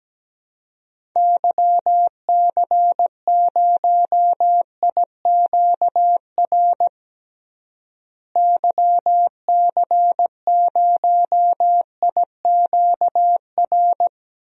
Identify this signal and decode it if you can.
Morse code, heard as a keyed tone, about 17 wpm, 700 Hz. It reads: YC0IQR YC0IQR
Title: DXCC-100 Call Signs - 6 Characters - Repeated 2x - 17wpm